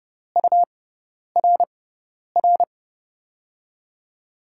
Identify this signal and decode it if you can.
Morse code, heard as a keyed tone, about 30 wpm, spaced as for 10 wpm, 700 Hz. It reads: URR
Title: USA Call Signs - 5 Characters - 6x Character Spacing - 30wpm